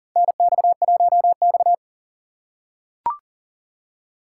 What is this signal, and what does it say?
Morse code, heard as a keyed tone, about 42 wpm, 690 Hz. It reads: NX1X
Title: USA Call Signs - 4 Characters - 6x Character Spacing - 40wpm